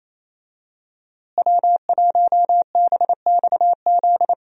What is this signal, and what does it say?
Morse code, heard as a keyed tone, about 28 wpm, 700 Hz. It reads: W1BXZ
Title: USA Call Signs - 5 Characters - Repeated 2x - 28wpm